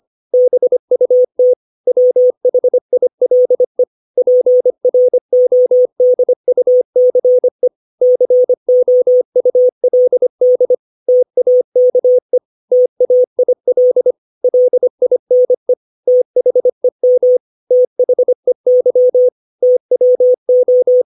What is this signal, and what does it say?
Morse code, heard as a keyed tone, about 25 wpm, 500 Hz. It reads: BUT WHILE PRODUCE COULD TAKE TAIL LINE THEM THEY TWO
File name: 632 words / 632 Words 25WPM